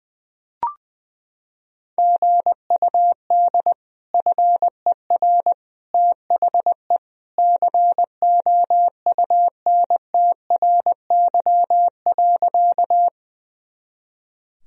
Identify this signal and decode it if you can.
Morse code, heard as a keyed tone, about 21 wpm, 700 Hz. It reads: GUD FER THE COUNTRY.
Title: Sentences from Top 300 Words and Abbreviations 20wpm